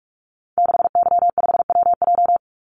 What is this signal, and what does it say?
Morse code, heard as a keyed tone, about 46 wpm, 700 Hz. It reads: 6Y5WJ
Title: Call Signs - 45wpm